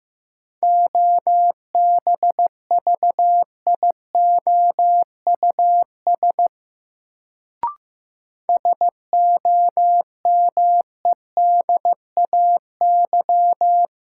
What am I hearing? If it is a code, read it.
Morse code, heard as a keyed tone, about 15 wpm, 700 Hz. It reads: OBVIOUS SOMEDAY
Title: Sentences from Top 1600 Words - New Word Drill Alt - 15wpm